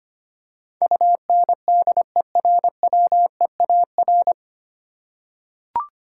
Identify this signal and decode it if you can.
Morse code, heard as a keyed tone, about 25 wpm, 700 Hz. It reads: UNDERWEAR
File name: Nine Letter Words - 25wpm